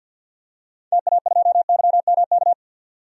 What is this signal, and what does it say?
Morse code, heard as a keyed tone, about 47 wpm, 700 Hz. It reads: TA2YMK
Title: DXCC-100 Call Signs - 6 Characters - 50wpm